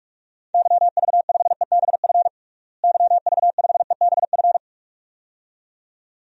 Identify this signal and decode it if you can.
Morse code, heard as a keyed tone, about 41 wpm, 700 Hz. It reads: YV5EBF YV5EBF
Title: DXCC-100 Call Signs - 6 Characters - Repeated 2x - 45wpm